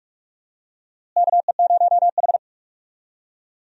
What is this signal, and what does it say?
Morse code, heard as a keyed tone, about 46 wpm, 700 Hz. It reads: KE0H
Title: USA Call Signs - 4 Characters - Repeated 2x - 45wpm